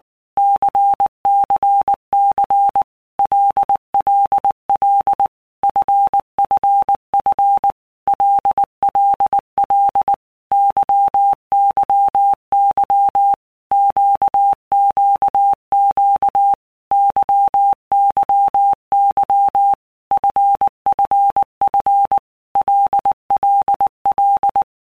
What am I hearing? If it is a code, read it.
Morse code, heard as a keyed tone, about 19 wpm, 800 Hz. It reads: CCC LLL FFF LLL YYY QQQ YYY FFF LLL
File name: Basic Letter Groups By Blocks / Groups@18 / LFCYQ@18